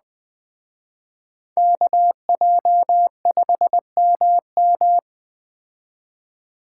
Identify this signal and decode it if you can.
Morse code, heard as a keyed tone, about 20 wpm, 700 Hz. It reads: KJ5MM